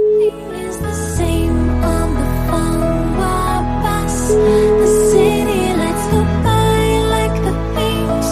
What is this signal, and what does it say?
Ringback tone (Europe, 425 Hz), with noise.